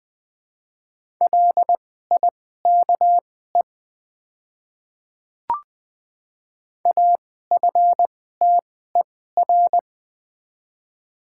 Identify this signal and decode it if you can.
Morse code, heard as a keyed tone, about 20 wpm, spaced as for 15 wpm, 700 Hz. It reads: LIKE AFTER
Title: 2x Character Spacing - Top 300 Words - 20wpm